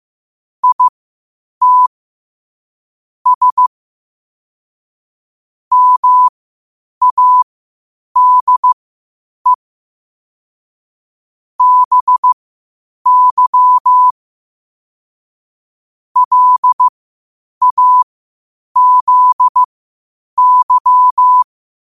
Morse code, heard as a keyed tone, about 15 wpm, spaced as for 8 wpm, 1000 Hz. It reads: IT S MADE BY LAZY
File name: qotdfiles / 8wpm